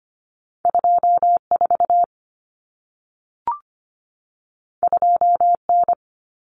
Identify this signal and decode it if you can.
Morse code, heard as a keyed tone, about 25 wpm, 700 Hz. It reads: 24 2N
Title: Common Cut Numbers - 2-Digit 25wpm